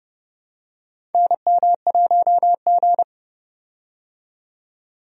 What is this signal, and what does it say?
Morse code, heard as a keyed tone, about 30 wpm, 700 Hz. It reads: NM1G
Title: USA Call Signs - 4 Characters - 30wpm